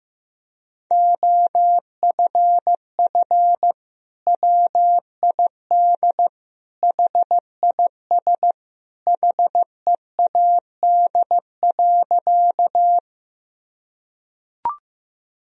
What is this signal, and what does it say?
Morse code, heard as a keyed tone, about 15 wpm, 700 Hz. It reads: OFF WID HIS HEAD.
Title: Sentences from Top 300 Words and Abbreviations 15wpm